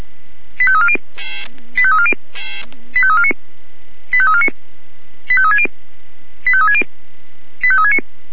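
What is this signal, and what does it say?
Five-tone selective call (ZVEI), heard as quick runs of five tones, tones of about 70 ms. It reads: 85340 85340 85349 85349 85360 85360 85369
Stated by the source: ZVEI five-tone selective call received on SDR.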